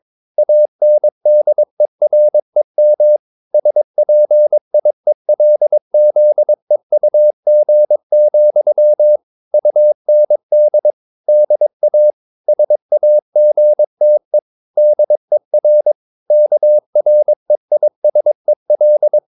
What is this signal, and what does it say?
Morse code, heard as a keyed tone, about 22 wpm, 600 Hz. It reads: ANDEREM SPIELZEUG, UND DA SAGTE DER KREISEL